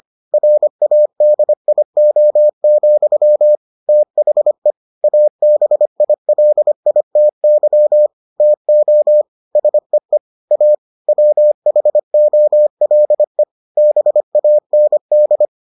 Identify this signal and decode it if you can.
Morse code, heard as a keyed tone, about 25 wpm, 600 Hz. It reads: RADIO, THE ABILITY TO SEE A WHOLE BAND